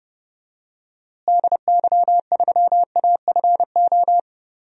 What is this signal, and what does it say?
Morse code, heard as a keyed tone, about 30 wpm, 700 Hz. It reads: DY3AFO
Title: DXCC-100 Call Signs - 6 Characters - Repeated 2x - 30wpm